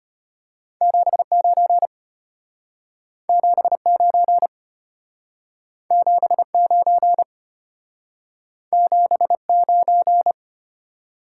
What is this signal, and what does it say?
Morse code, heard as a keyed tone, about 32 wpm, 700 Hz. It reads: 79 79 79 79